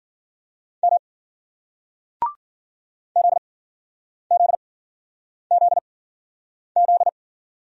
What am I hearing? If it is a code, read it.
Morse code, heard as a keyed tone, about 51 wpm, spaced as for 9 wpm, 700 Hz. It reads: M ZZZZ